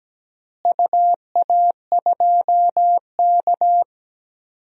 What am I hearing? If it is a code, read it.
Morse code, heard as a keyed tone, about 17 wpm, 700 Hz. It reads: UA2K
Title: DXCC-100 Call Signs - 4 Characters - Repeated 3x - 17wpm